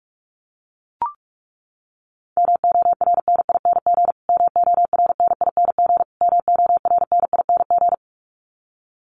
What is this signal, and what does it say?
Morse code, heard as a keyed tone, about 44 wpm, 700 Hz. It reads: MORNING MORNING MORNING
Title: Top 500 Words Repeated 45wpm